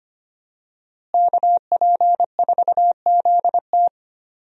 Morse code, heard as a keyed tone, about 25 wpm, 700 Hz. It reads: KP4ZT
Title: USA Call Signs - 5 Characters - Repeated 2x - 25wpm